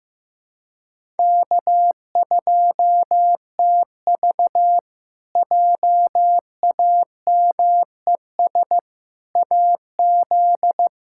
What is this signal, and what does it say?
Morse code, heard as a keyed tone, about 15 wpm, 700 Hz. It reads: K2TV JAMES AZ
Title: K1USN SST QSO Practice - 15wpm